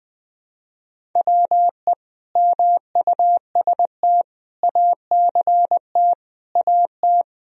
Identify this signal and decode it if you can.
Morse code, heard as a keyed tone, about 20 wpm, 700 Hz. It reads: WE MUST ACT AT
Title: Sentences from Top 1000 Words - Review Part 1 - 20wpm